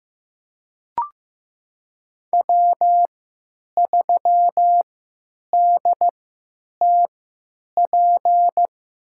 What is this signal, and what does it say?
Morse code, heard as a keyed tone, about 15 wpm, spaced as for 9 wpm, 700 Hz. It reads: W3DTP
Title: USA Call Signs - 5 Characters - 3x Character Spacing - 15wpm